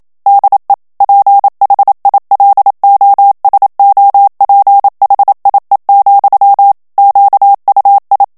Morse code, heard as a keyed tone, about 27 wpm, 800 Hz. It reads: DE PHILOSOPHIE, QUI
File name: FRANCAIS TEXTE CLAIR 2 VITESSE 26 MOTS MINUTES